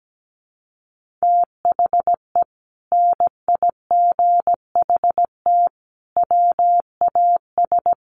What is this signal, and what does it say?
Morse code, heard as a keyed tone, about 17 wpm, 700 Hz. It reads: THE NIGHT WAS
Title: Sentences from Top 1500 Words 17wpm